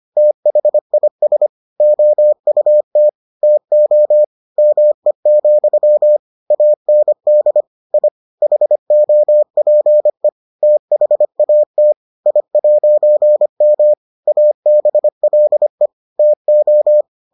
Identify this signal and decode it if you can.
Morse code, heard as a keyed tone, about 25 wpm, 600 Hz. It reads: THIS OUT TO ME, AND I HOPE THAT I'M ABLE TO